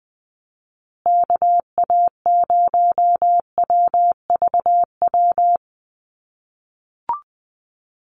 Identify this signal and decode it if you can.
Morse code, heard as a keyed tone, about 20 wpm, 700 Hz. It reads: KA0WVW
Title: USA Call Signs Repeated 3x 20wpm